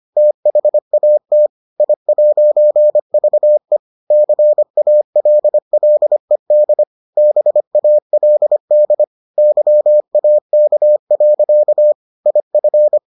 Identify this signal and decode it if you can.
Morse code, heard as a keyed tone, about 25 wpm, 600 Hz. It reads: THAT I'VE CALLED BALD YAK. IF